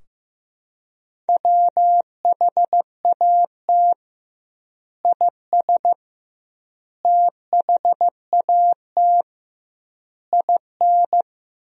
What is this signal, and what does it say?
Morse code, heard as a keyed tone, about 15 wpm, 700 Hz. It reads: WHAT IS THAT IN
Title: 2x Word-Spacing - Sentences from Top 200 Words - 15wpm